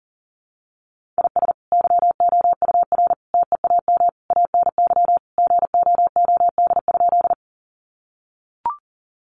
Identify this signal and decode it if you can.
Morse code, heard as a keyed tone, about 40 wpm, 700 Hz. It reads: IS YOUR TEAM ANY GOOD?
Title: Sentences from Top 1500 Words 40wpm